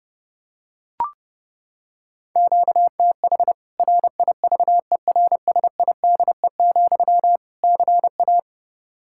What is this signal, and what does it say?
Morse code, heard as a keyed tone, about 32 wpm, 700 Hz. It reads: QTH RIVERSIDE, CA
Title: K1USN SST QSO Practice 30wpm